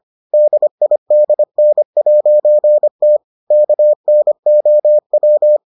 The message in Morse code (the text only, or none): DIDN'T KNOW